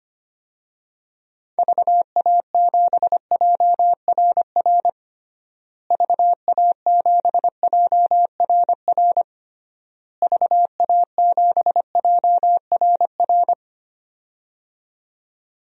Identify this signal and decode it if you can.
Morse code, heard as a keyed tone, about 25 wpm, 700 Hz. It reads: VA7JRR VA7JRR VA7JRR
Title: DXCC-100 Call Signs - 6 Characters - Repeated 3x - 25wpm